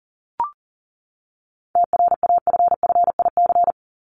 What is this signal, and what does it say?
Morse code, heard as a keyed tone, about 40 wpm, 700 Hz. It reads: TRAFFIC
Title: Top 1251-1500 Words 40wpm